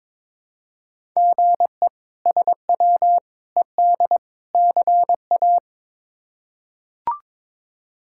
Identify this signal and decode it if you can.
Morse code, heard as a keyed tone, about 22 wpm, 700 Hz. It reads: GE SW ED CA
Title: K1USN SST QSO Practice - 22wpm